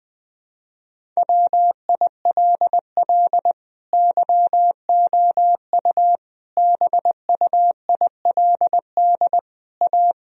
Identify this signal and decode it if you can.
Morse code, heard as a keyed tone, about 20 wpm, 700 Hz. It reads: WILL YOU BUILD A
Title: Sentences from Top 1000 Words - Review Part 1 - 20wpm